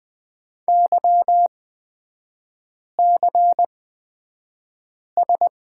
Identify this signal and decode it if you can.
Morse code, heard as a keyed tone, about 20 wpm, spaced as for 5 wpm, 700 Hz. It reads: YCS
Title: 255 Lesson 41e - Period - Three Characters 20wpm